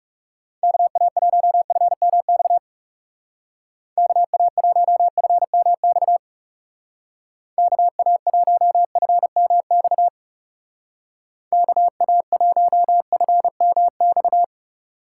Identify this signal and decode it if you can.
Morse code, heard as a keyed tone, about 36 wpm, 700 Hz. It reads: KA1FMX KA1FMX KA1FMX KA1FMX
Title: Speed Racing - USA 2x3 Call Signs 30wpm